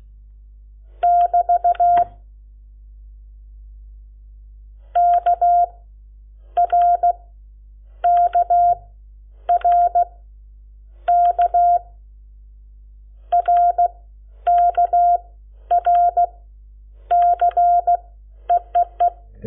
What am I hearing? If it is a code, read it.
Morse code, heard as a keyed tone, about 16 wpm, 700 Hz. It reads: <BT> K R K R K R K R C EEE
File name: Lecciones / CLASE Nº7